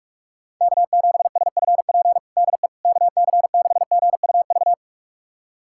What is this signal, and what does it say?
Morse code, heard as a keyed tone, about 46 wpm, 700 Hz. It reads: K7SFP DE KC6GUV
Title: USA Callsign DE Callsign 45wpm